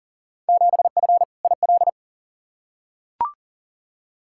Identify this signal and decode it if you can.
Morse code, heard as a keyed tone, about 37 wpm, 700 Hz. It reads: 7F IL
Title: Field Day Exchanges Encore - 40wpm